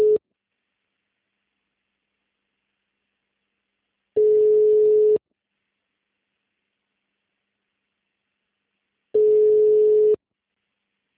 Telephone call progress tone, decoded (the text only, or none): ringback tone